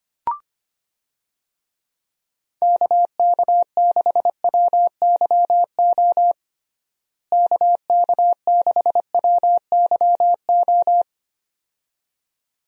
Morse code, heard as a keyed tone, about 25 wpm, 700 Hz. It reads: KK6WYO KK6WYO
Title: USA Call Signs - 6 Characters - Repeated 2x - 25wpm